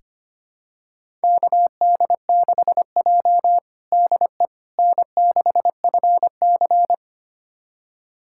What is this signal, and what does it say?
Morse code, heard as a keyed tone, about 25 wpm, 700 Hz. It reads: KD6J DE N6FC
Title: USA Callsign DE Callsign - 4 Characters - 25wpm